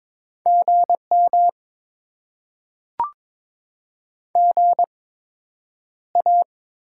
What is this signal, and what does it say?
Morse code, heard as a keyed tone, about 22 wpm, spaced as for 9 wpm, 700 Hz. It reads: GM GA
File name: Top 20 QSO Elements - 8x Character Spacing - 22wpm